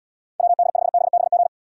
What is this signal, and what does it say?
Morse code, heard as a keyed tone, about 77 wpm, 700 Hz. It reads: WD5BBG